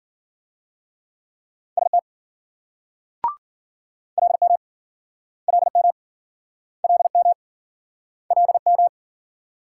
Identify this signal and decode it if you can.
Morse code, heard as a keyed tone, about 53 wpm, 700 Hz. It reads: ST LM LM LM LM